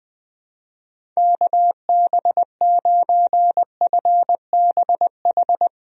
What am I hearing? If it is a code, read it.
Morse code, heard as a keyed tone, about 20 wpm, 700 Hz. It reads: KB9FBH
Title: USA Call Signs - 6 Characters - Repeated 2x - 20wpm